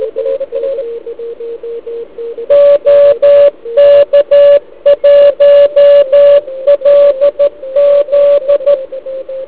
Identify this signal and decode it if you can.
Morse code, heard as a keyed tone, about 13 wpm, 1140 Hz. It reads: OK1LZ